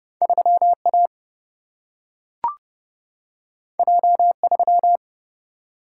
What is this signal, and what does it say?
Morse code, heard as a keyed tone, about 30 wpm, 700 Hz. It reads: 3 A J 3
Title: Single-Letter Single-Number 30wpm